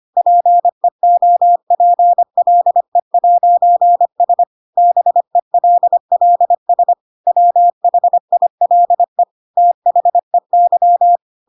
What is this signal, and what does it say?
Morse code, heard as a keyed tone, about 25 wpm, 700 Hz. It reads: PEOPLE'S BELLS WHILE THEY